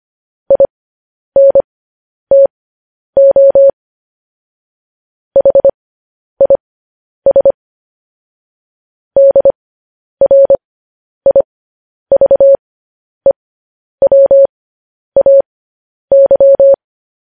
Morse code, heard as a keyed tone, about 25 wpm, spaced as for 10 wpm, 550 Hz. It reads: INTO HIS DRIVEWAY